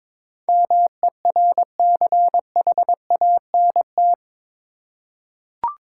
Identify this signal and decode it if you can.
Morse code, heard as a keyed tone, about 22 wpm, 700 Hz. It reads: MERCHANT
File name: practice / 22wpm / Eight Letter Words - 22wpm